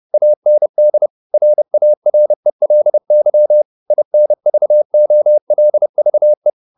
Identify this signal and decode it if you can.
Morse code, heard as a keyed tone, about 30 wpm, 600 Hz. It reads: AND RARELY INVOLVE